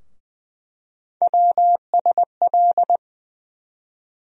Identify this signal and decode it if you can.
Morse code, heard as a keyed tone, about 20 wpm, 700 Hz. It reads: WSL
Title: USA Call Sign Postfixes - 3 Letters 20wpm